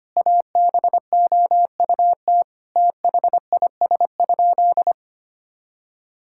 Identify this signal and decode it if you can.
Morse code, heard as a keyed tone, about 25 wpm, 700 Hz. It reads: ABOUT THIS?
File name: practice / 25wpm / Sentences with 4-Word Phrases - 25wpm